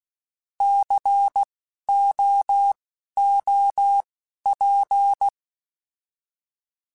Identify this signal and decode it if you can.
Morse code, heard as a keyed tone, about 16 wpm, spaced as for 12 wpm, 800 Hz. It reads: COOP
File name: Alphabetical / Alphabetical mp3s / Cs @10